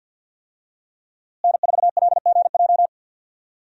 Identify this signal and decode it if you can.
Morse code, heard as a keyed tone, about 50 wpm, 700 Hz. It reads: N4LGJ
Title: USA Call Signs - 5 Characters - Repeated 3x - 50wpm